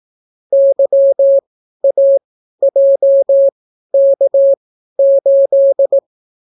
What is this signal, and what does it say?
Morse code, heard as a keyed tone, about 18 wpm, spaced as for 12 wpm, 550 Hz. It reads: YAJK8